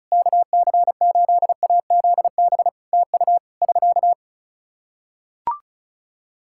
Morse code, heard as a keyed tone, about 36 wpm, 700 Hz. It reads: KC8AZB TU <SK>
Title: Practice QSO - Exchange 4 - Ending - 35wpm